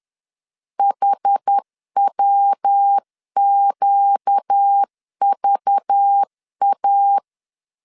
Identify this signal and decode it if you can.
Morse code, heard as a keyed tone, about 10 wpm, 800 Hz. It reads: HWQVA
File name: LEZIONI / 9 286 10wpm 2735